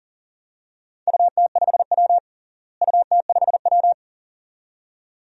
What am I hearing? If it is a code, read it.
Morse code, heard as a keyed tone, about 40 wpm, 700 Hz. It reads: UT5W UT5W
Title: DXCC-100 Call Signs - 4 Characters - Repeated 2x - 40wpm